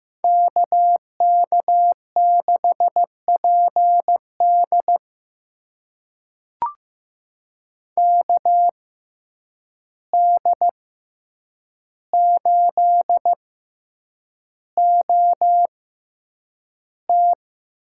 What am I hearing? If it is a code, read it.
Morse code, heard as a keyed tone, about 15 wpm, spaced as for 8 wpm, 700 Hz. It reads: KK6PD KD8OT
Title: USA Call Signs - 5 Characters - 6x Character Spacing - 15wpm